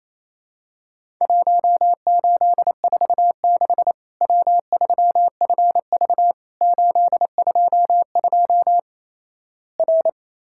Morse code, heard as a keyed tone, about 27 wpm, 700 Hz. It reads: 1846 W3FV 822 R
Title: Worked All Europe DX Contest - QTC Exchange - 28wpm